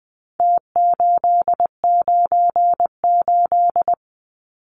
Morse code, heard as a keyed tone, about 20 wpm, 700 Hz. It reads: T898
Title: Numbers - Contest Style - 4-Digit 20wpm